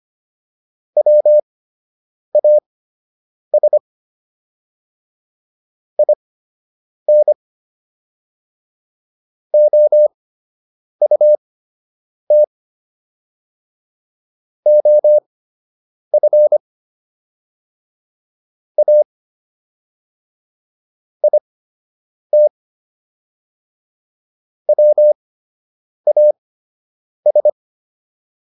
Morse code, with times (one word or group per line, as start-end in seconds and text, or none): 1.0-3.8 WAS
6.0-7.3 IN
9.5-12.4 OUT
14.7-16.6 OF
18.8-19.0 A
21.2-22.5 IT
24.7-27.5 WAS